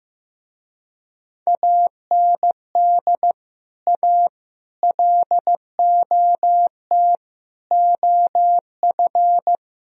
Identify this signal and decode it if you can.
Morse code, heard as a keyed tone, about 15 wpm, 700 Hz. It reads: AND A LOT OF
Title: Speed-Racing - Sets of 5 Words 15wpm